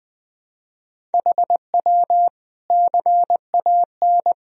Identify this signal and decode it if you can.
Morse code, heard as a keyed tone, about 20 wpm, 700 Hz. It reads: HW CAN
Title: Sentences from Top 500 Words and Abbreviations 20wpm